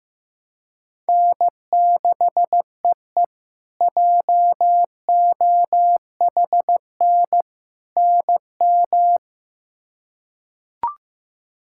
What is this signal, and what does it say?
Morse code, heard as a keyed tone, about 15 wpm, 700 Hz. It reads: N6EE JOHN NM
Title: K1USN SST QSO Practice - 15wpm